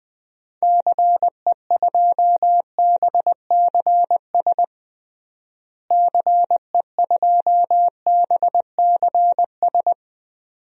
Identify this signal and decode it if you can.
Morse code, heard as a keyed tone, about 20 wpm, 700 Hz. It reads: CE2BCS CE2BCS
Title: DXCC-100 Call Signs - 6 Characters - Repeated 2x - 20wpm